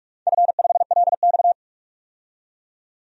Morse code, heard as a keyed tone, about 44 wpm, 700 Hz. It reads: F5LX
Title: DXCC-100 Call Signs - 4 Characters - 45wpm